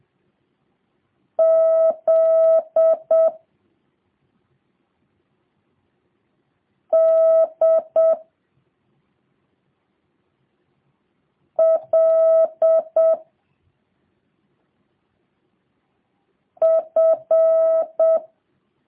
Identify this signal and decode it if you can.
Morse code, heard as a keyed tone, about 7 wpm, 650 Hz. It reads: Z D L F